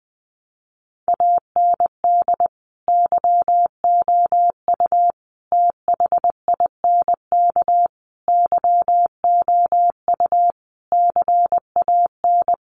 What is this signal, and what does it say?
Morse code, heard as a keyed tone, about 20 wpm, 700 Hz. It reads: AND YOU THINK YOU CAN
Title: Sentences from Top 1300 Words 20wpm